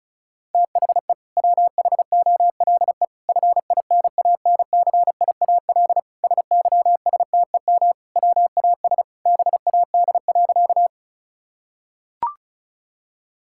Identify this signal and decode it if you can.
Morse code, heard as a keyed tone, about 34 wpm, 700 Hz. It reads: THE WHOLE FINANCIAL SYSTEM WAS BAD.
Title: Sentences from Top 1000 Words - Review Part 5 - 35wpm